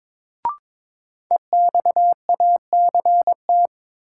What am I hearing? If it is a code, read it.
Morse code, heard as a keyed tone, about 21 wpm, 700 Hz. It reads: EXACT
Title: Sentences from Top 1300 Words - New Word Drill - 22wpm